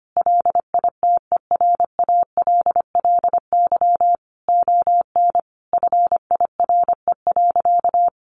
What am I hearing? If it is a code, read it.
Morse code, heard as a keyed tone, about 25 wpm, 700 Hz. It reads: LITERALLY ON FIRE.
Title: Sentences from Top 1400 Words 25wpm